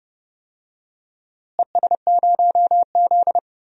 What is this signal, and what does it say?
Morse code, heard as a keyed tone, about 31 wpm, 700 Hz. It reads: ES0Z
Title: DXCC-100 Call Signs - 4 Characters - Repeated 2x - 30wpm